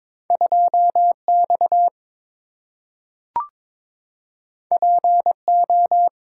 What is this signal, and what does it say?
Morse code, heard as a keyed tone, about 22 wpm, 700 Hz. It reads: 2X PO
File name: practice / 22wpm / Two Letter-Number Combinations - 22wpm